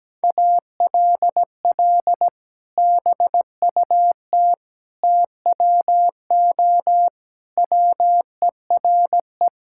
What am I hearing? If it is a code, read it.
Morse code, heard as a keyed tone, about 17 wpm, 700 Hz. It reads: ALL BUT TWO WERE
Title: Sentences from Top 1000 Words - Review Part 5 - 17wpm